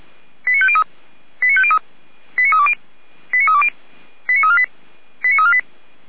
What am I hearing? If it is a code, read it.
Five-tone selective call (ZVEI), heard as quick runs of five tones, tones of about 70 ms. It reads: 89503 89503 89320 89320 89348 89348